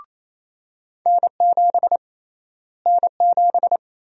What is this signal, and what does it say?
Morse code, heard as a keyed tone, about 28 wpm, 700 Hz. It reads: N7 N7
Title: USA Call Sign Prefixes - Repeated 2x - 28wpm